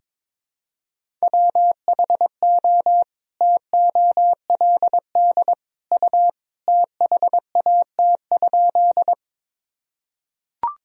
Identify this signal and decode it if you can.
Morse code, heard as a keyed tone, about 22 wpm, 700 Hz. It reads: WHO TOLD U THAT?
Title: Sentences from Top 400 Words and Abbreviations 22wpm